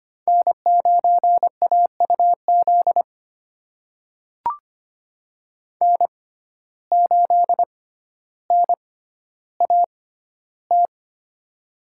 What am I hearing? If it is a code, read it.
Morse code, heard as a keyed tone, about 25 wpm, spaced as for 13 wpm, 700 Hz. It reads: N9AUZ N8NAT
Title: USA Call Signs - 5 Characters - 6x Character Spacing - 25wpm